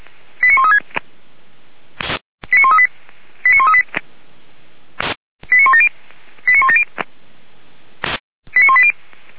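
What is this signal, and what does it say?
Five-tone selective call (ZVEI), heard as quick runs of five tones, tones of about 70 ms. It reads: 89136 89138 89138 89170 89170 89180
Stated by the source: ZVEI five-tone selective call received by SDR, audio archive of Feuerwehr Orlamünde (f85360).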